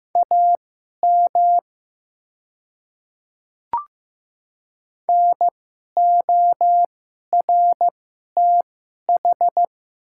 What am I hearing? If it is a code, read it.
Morse code, heard as a keyed tone, about 15 wpm, spaced as for 11 wpm, 700 Hz. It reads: AM NORTH
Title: 2x Character Spacing - Top 500 Words - 15wpm